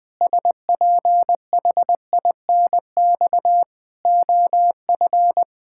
Morse code, heard as a keyed tone, about 20 wpm, 700 Hz. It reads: SPHINX OF
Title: Pangram Challenge v2 - 20wpm